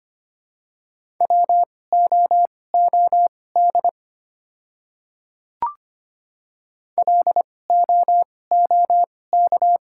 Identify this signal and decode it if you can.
Morse code, heard as a keyed tone, about 25 wpm, spaced as for 18 wpm, 700 Hz. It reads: WOOD LOOK
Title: 2x Character Spacing - Top 300 Words - 25wpm